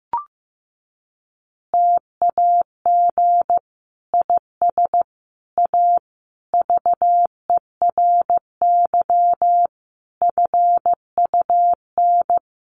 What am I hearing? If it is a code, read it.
Morse code, heard as a keyed tone, about 15 wpm, 700 Hz. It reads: TAG IS A VERY FUN
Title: Additional Easy Phrases 15wpm